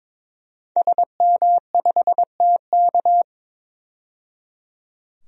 Morse code, heard as a keyed tone, about 22 wpm, 700 Hz. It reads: SM5TK
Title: DXCC-100 Call Signs - 5 Characters - 22wpm